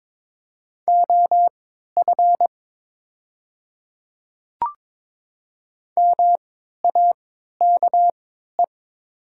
Morse code, heard as a keyed tone, about 22 wpm, spaced as for 12 wpm, 700 Hz. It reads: OF MAKE
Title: 3x Character Spacing - Top 500 Words - 22wpm